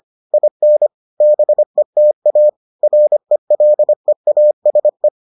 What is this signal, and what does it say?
Morse code, heard as a keyed tone, about 25 wpm, 600 Hz. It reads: IN BETA RELEASE